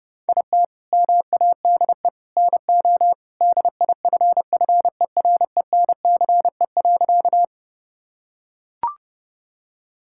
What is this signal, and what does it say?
Morse code, heard as a keyed tone, about 32 wpm, 700 Hz. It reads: IT MADE NO DIFFERENCE.
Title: Sentences from Top 1000 Words - Review Part 5 - 30wpm